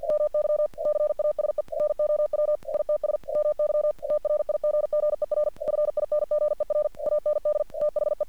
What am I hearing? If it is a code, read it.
Morse code, heard as a keyed tone, about 48 wpm, 600 Hz. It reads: MY CASE NOW ITS MY TRIGGER FINGER AND THE
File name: archived / CRSnet